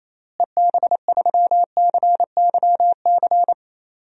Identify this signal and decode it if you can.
Morse code, heard as a keyed tone, about 28 wpm, 700 Hz. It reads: EB3CYC
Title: DXCC-100 Call Signs - 6 Characters - Repeated 3x - 28wpm